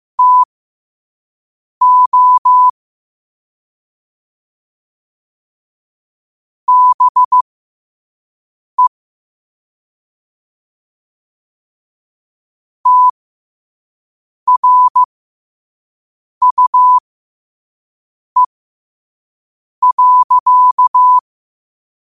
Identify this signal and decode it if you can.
Morse code, heard as a keyed tone, about 15 wpm, spaced as for 5 wpm, 1000 Hz. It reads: TO BE TRUE.
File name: qotdfiles / 5wpm